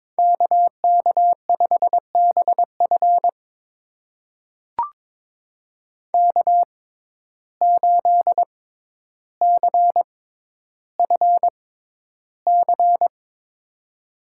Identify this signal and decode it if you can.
Morse code, heard as a keyed tone, about 22 wpm, spaced as for 11 wpm, 700 Hz. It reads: KK5BF K8CFC
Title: USA Call Signs - 6x Character Spacing - 22wpm